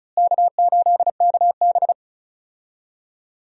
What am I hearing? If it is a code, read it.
Morse code, heard as a keyed tone, about 36 wpm, 700 Hz. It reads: K8KB
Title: USA Call Signs - 4 Characters - Repeated 3x - 35wpm